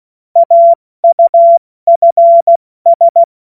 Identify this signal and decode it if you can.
Morse code, heard as a keyed tone, about 16 wpm, 680 Hz. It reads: AUFS
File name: Morsekurs / Witze / Witz 29. November 2022 70 F80